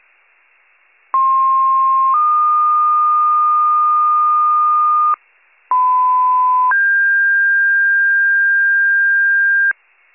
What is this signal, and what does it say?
Two-tone page, heard as a one-second tone, then a three-second tone, about 1060 then 1220 Hz, about 1010 then 1600 Hz. It is heard from 1.1 to 9.7 s.